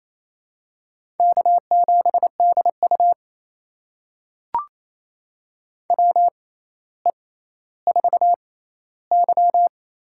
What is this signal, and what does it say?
Morse code, heard as a keyed tone, about 27 wpm, spaced as for 14 wpm, 700 Hz. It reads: K7DU WE4Y
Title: USA Call Signs - 6x Character Spacing - 28wpm